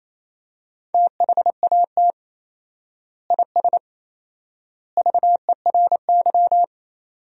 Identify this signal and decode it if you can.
Morse code, heard as a keyed tone, about 28 wpm, 700 Hz. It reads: THAT IS VERY
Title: 4x Word-Spacing - Sentences from Top 400 Words - 28wpm